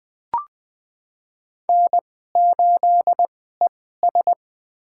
Morse code, heard as a keyed tone, about 21 wpm, spaced as for 15 wpm, 700 Hz. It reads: N8ES
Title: USA Call Signs - 4 Characters - 2x Character Spacing - 20wpm